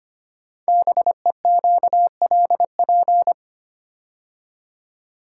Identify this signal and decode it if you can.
Morse code, heard as a keyed tone, about 25 wpm, 700 Hz. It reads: BEQLP
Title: Code Groups - 5 Letters - 25wpm